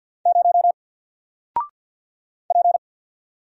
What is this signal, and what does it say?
Morse code, heard as a keyed tone, about 50 wpm, 700 Hz. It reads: 0 P
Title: Single Character (Letter:Number:Punctuation:Prosign) 50wpm